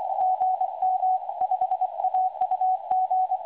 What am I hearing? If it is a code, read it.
Morse code, heard as a keyed tone, about 23 wpm, 750 Hz. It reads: GM5AUG
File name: websdr recording 2023-04-10t12-24-59z 5260.0khz